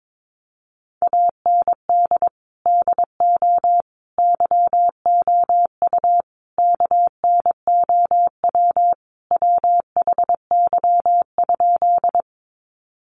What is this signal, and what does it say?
Morse code, heard as a keyed tone, about 22 wpm, 700 Hz. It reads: AND DO YOU KNOW WHY?
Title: Sentences from Top 200 Words 22wpm